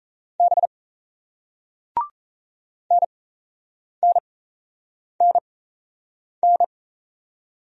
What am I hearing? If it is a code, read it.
Morse code, heard as a keyed tone, about 37 wpm, spaced as for 8 wpm, 700 Hz. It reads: B NNNN